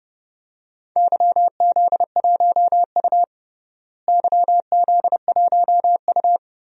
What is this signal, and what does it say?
Morse code, heard as a keyed tone, about 29 wpm, 700 Hz. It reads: YZ1U YZ1U
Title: DXCC-100 Call Signs - 4 Characters - Repeated 3x - 30wpm